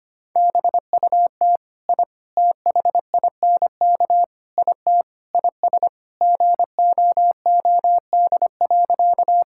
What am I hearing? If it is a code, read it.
Morse code, heard as a keyed tone, about 25 wpm, 700 Hz. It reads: BUT I THINK IT IS GOOD.
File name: practice / 25wpm / Sentences with 4-Word Phrases - 25wpm